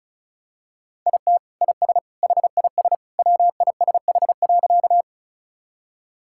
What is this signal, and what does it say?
Morse code, heard as a keyed tone, about 34 wpm, 700 Hz. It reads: IT IS HIS WISH.
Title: Sentences from Top 1600 Words 35wpm